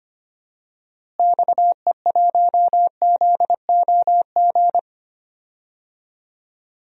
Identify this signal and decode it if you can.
Morse code, heard as a keyed tone, about 25 wpm, 700 Hz. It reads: XE1ZOG
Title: DXCC-100 Call Signs - 6 Characters - Repeated 3x - 25wpm